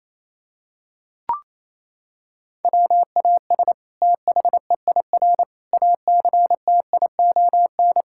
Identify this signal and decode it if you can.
Morse code, heard as a keyed tone, about 27 wpm, 700 Hz. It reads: WAS THEIR ACTION
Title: Sentences from Top 1000 Words - Review Part 6 - 28wpm